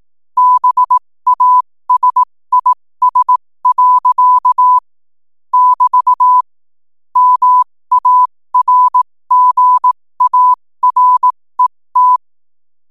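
Morse code, heard as a keyed tone, about 18 wpm, spaced as for 15 wpm, 1000 Hz. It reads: BASIS. <BT> MARGARET
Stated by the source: 15 wpm Morse code from RNZN Communicators Association.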